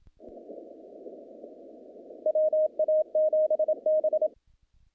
Morse code, heard as a keyed tone, about 27 wpm, 620 Hz. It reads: WA7B
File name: contests / naqp-cw-k5tr-op-k5pi- -2200